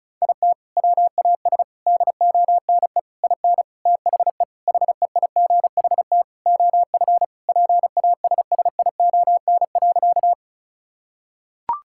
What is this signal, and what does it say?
Morse code, heard as a keyed tone, about 34 wpm, 700 Hz. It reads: IT WAS DONE IN THE HEIGHT OF PASSION.
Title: Sentences from Top 1600 Words 35wpm